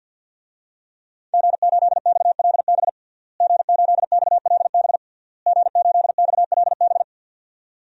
Morse code, heard as a keyed tone, about 48 wpm, 700 Hz. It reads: G8XLB G8XLB G8XLB
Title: DXCC-100 Call Signs - 5 Characters - Repeated 3x - 50wpm